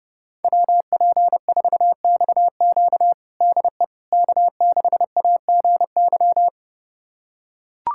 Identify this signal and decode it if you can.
Morse code, heard as a keyed tone, about 30 wpm, 700 Hz. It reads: WP4XQ DE K6AGY
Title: USA Callsign DE Callsign 30wpm